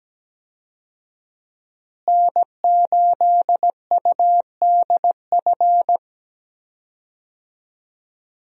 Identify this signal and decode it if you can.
Morse code, heard as a keyed tone, about 17 wpm, 700 Hz. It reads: N8UDF
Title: USA Call Signs - 5 Characters - Repeated 2x - 17wpm